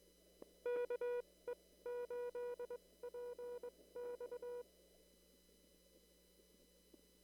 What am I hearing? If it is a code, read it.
Morse code, heard as a keyed tone, about 22 wpm, spaced as for 17 wpm, 500 Hz. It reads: KE8PX
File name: CW / kt5j wpx cw-op-k5pi